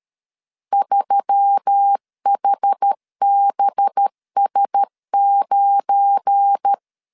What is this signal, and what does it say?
Morse code, heard as a keyed tone, about 13 wpm, 800 Hz. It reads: 3HBS9
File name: LEZIONI / 7 GF9Z / 7 GF9 12wpm 2319